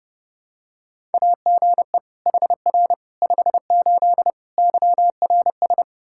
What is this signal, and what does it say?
Morse code, heard as a keyed tone, about 30 wpm, 700 Hz. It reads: AGE HR 58 YRS
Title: Practice QSO - Exchange 3 - More Info - 30wpm